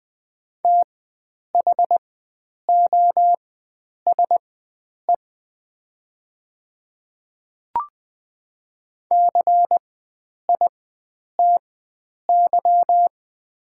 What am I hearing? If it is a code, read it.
Morse code, heard as a keyed tone, about 20 wpm, spaced as for 9 wpm, 700 Hz. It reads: THOSE CITY
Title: 4x Character Spacing - Top 300 Words - 20wpm